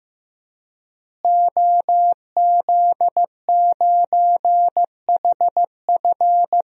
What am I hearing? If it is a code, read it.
Morse code, heard as a keyed tone, about 15 wpm, 700 Hz. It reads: OZ9HF
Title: DXCC-100 Call Signs - 5 Characters - Repeated 3x - 15wpm